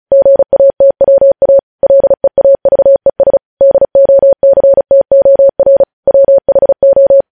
Morse code, heard as a keyed tone, about 36 wpm, 550 Hz. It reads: GATWA LEAVES DOCTOR WHO
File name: podcast / News headlines-1x-35x35